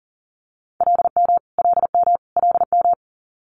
Morse code, heard as a keyed tone, about 40 wpm, 690 Hz. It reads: LM LM LM
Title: Multi-letter Phonemes Repeated 3x 40wpm